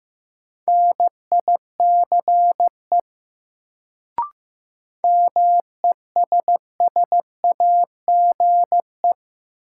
Morse code, heard as a keyed tone, about 15 wpm, 700 Hz. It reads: NICE MESSAGE
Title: Sentences from Top 1100 Words - New Word Drill Alt - 15wpm